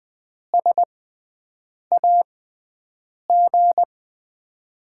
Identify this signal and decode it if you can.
Morse code, heard as a keyed tone, about 20 wpm, spaced as for 7 wpm, 700 Hz. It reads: SAG
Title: USA Call Signs - 5 Characters - 6x Character Spacing - 20wpm